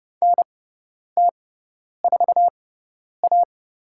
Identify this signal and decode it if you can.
Morse code, heard as a keyed tone, about 32 wpm, spaced as for 10 wpm, 700 Hz. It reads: NT4A
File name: Lesson 17g - Letter u - Call Signs 30wpm